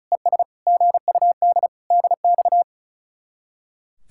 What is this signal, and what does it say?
Morse code, heard as a keyed tone, about 47 wpm, spaced as for 42 wpm, 700 Hz. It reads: ES GUD DX
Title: Speed-Racing Common QSO Phrases - 35wpm